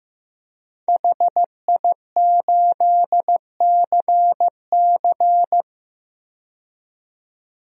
Morse code, heard as a keyed tone, about 15 wpm, 700 Hz. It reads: HI8CC
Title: DXCC-100 Call Signs - 5 Characters - 15wpm